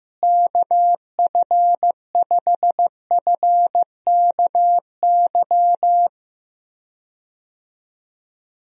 Morse code, heard as a keyed tone, about 15 wpm, 700 Hz. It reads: KF5FKY
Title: USA Call Signs - 6 Characters - Repeated 2x - 15wpm